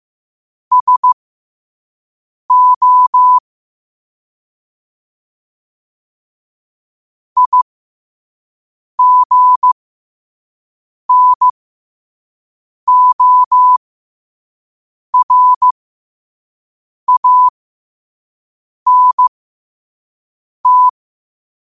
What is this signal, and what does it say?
Morse code, heard as a keyed tone, about 15 wpm, spaced as for 5 wpm, 1000 Hz. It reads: SO IGNORANT